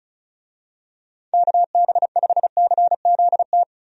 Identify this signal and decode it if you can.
Morse code, heard as a keyed tone, about 34 wpm, 700 Hz. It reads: KB5CZT
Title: USA Call Signs - 6 Characters - Repeated 3x - 35wpm